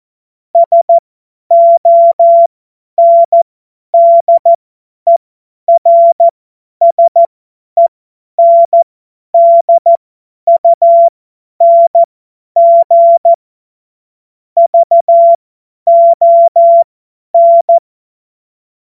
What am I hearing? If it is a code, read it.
Morse code, heard as a keyed tone, about 14 wpm, spaced as for 10 wpm, 680 Hz. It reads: SONDERSENDUNG VON
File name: Morsekurs / Witze / Witz 09. November 2021, 50 F70